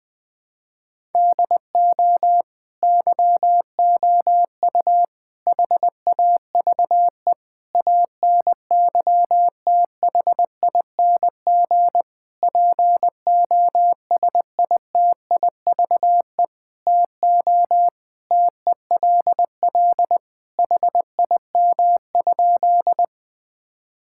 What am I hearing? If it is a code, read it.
Morse code, heard as a keyed tone, about 20 wpm, 700 Hz. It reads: DO YOU HAVE ANYTHING POSITIVE TO TELL HIM?
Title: Sentences from Top 1000 Words - Review Part 5 - 20wpm